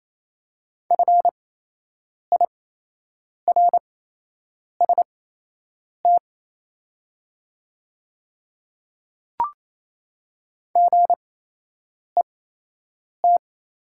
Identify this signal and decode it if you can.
Morse code, heard as a keyed tone, about 27 wpm, spaced as for 8 wpm, 700 Hz. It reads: FIRST GET
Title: 8x Character Spacing - Top 200 Words - 28wpm